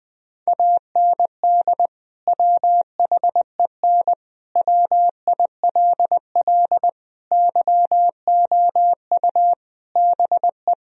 Morse code, heard as a keyed tone, about 20 wpm, 700 Hz. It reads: AND WHEN WILL YOU BE
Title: Sentences from Top 1000 Words - Review Part 2 - 20wpm